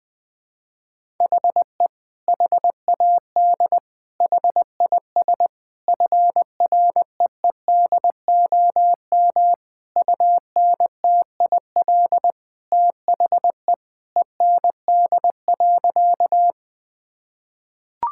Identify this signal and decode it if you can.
Morse code, heard as a keyed tone, about 20 wpm, 700 Hz. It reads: HE HAD HIS FREEDOM UNTIL THE END.
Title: Sentences from Top 1000 Words - Review Part 5 - 20wpm